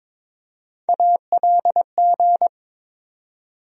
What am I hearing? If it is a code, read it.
Morse code, heard as a keyed tone, about 22 wpm, 700 Hz. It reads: ALG
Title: USA Call Sign Postfixes - 3 Letters 22wpm